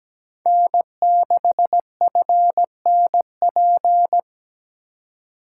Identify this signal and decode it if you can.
Morse code, heard as a keyed tone, about 20 wpm, 700 Hz. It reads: N6FNP
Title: Speed Racing - USA 1x3 Call Signs 17wpm